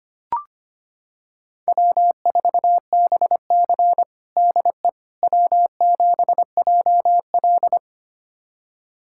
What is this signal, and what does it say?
Morse code, heard as a keyed tone, about 25 wpm, 700 Hz. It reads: W4BC DE W7JL
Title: USA Callsign DE Callsign - 4 Characters - 25wpm